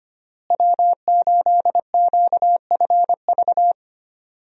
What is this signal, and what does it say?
Morse code, heard as a keyed tone, about 25 wpm, 700 Hz. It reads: W8QFV
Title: USA Call Signs - 5 Characters - Repeated 3x - 25wpm